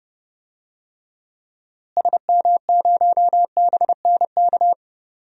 Morse code, heard as a keyed tone, about 32 wpm, 700 Hz. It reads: SM0BNK
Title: DXCC-100 Call Signs - 6 Characters - Repeated 3x - 30wpm